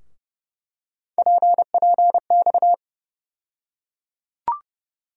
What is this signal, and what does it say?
Morse code, heard as a keyed tone, about 30 wpm, 700 Hz. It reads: PPX
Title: USA Call Sign Postfixes - 3 Letters 30wpm